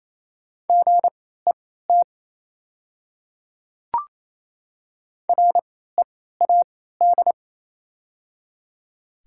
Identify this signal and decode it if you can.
Morse code, heard as a keyed tone, about 28 wpm, spaced as for 16 wpm, 700 Hz. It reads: GET READ
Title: 3x Character Spacing - Top 200 Words - 28wpm